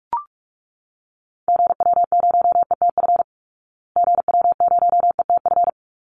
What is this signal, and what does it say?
Morse code, heard as a keyed tone, about 44 wpm, 700 Hz. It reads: GW0ETF GW0ETF
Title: Call Signs Repeated 3x - 45wpm